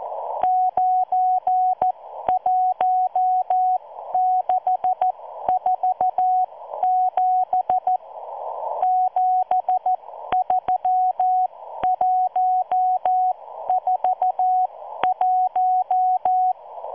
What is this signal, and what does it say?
Morse code, heard as a keyed tone, about 14 wpm, spaced as for 12 wpm, 740 Hz. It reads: 91647 73141